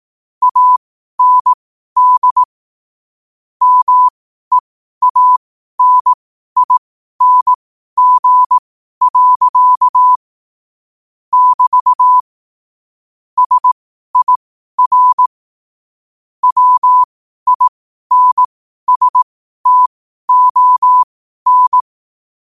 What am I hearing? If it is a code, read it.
Morse code, heard as a keyed tone, about 18 wpm, spaced as for 12 wpm, 1000 Hz. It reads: AND MEANING. <BT> SIR WINSTON
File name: qotdfiles / 12wpm